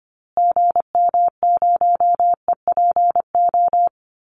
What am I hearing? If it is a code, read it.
Morse code, heard as a keyed tone, about 25 wpm, 700 Hz. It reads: GM0EPO
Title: Call Signs Repeated 2x - 25wpm